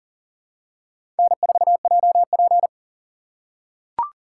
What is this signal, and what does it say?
Morse code, heard as a keyed tone, about 40 wpm, 700 Hz. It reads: N4JP
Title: USA Call Signs - 4 Characters - 4x Character Spacing - 40wpm